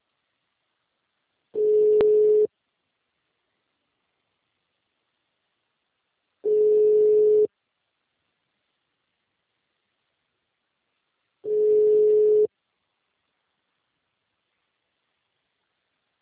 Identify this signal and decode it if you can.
Ringback tone (Europe, 425 Hz).